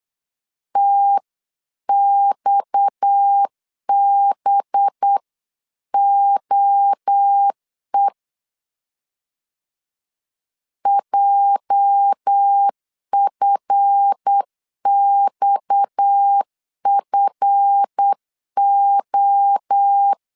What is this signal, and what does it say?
Morse code, heard as a keyed tone, about 8 wpm, 800 Hz. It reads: TXBOE JFXFO
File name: LEZIONI / 8 YJ45 / 8 YJ 8wpm 3313